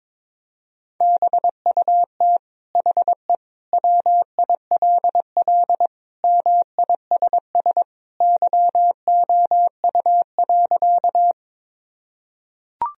Morse code, heard as a keyed tone, about 22 wpm, 700 Hz. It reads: BUT HE WILL MISS YOU.
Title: Sentences from Top 1000 Words - Review Part 3 - 22wpm